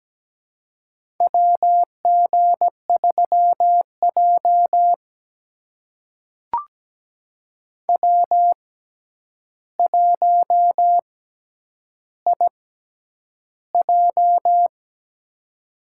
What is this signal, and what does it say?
Morse code, heard as a keyed tone, about 17 wpm, spaced as for 9 wpm, 700 Hz. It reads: WG3J W1IJ
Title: USA Call Signs - 6x Character Spacing - 17wpm